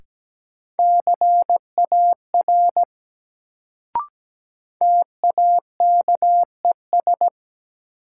Morse code, heard as a keyed tone, about 17 wpm, 700 Hz. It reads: CAR TAKES
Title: Sets of 5 Words Encore - New Word Drill - 17wpm